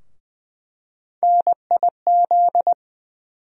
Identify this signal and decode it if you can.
Morse code, heard as a keyed tone, about 20 wpm, 700 Hz. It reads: NIZ
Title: USA Call Sign Postfixes - 3 Letters 20wpm